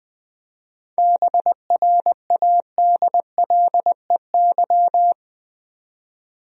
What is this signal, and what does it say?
Morse code, heard as a keyed tone, about 20 wpm, 700 Hz. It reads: BRADLEY
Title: USA Common Names - 20wpm